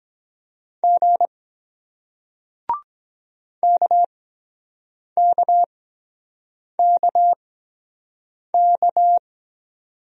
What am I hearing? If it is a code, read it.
Morse code, heard as a keyed tone, about 23 wpm, spaced as for 7 wpm, 700 Hz. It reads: G KKKK